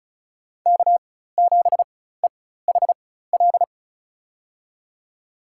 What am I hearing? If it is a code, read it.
Morse code, heard as a keyed tone, about 36 wpm, spaced as for 16 wpm, 700 Hz. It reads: K7EHL
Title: USA Call Signs - 5 Characters - 4x Character Spacing - 35wpm